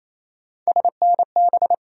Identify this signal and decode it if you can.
Morse code, heard as a keyed tone, about 28 wpm, 700 Hz. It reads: SNB
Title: USA Call Sign Postfixes - 3 Letters 28wpm